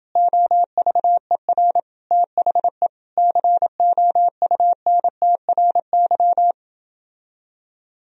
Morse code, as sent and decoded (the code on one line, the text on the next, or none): --- ...- . .-. / - .... / -.-. --- ..- -. - .-. -.--
OVER THE COUNTRY